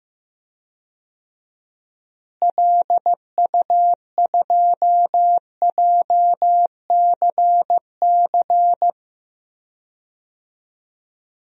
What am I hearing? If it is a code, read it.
Morse code, heard as a keyed tone, about 15 wpm, 700 Hz. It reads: LU2JCC
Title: DXCC-100 Call Signs - 6 Characters - Repeated 2x - 15wpm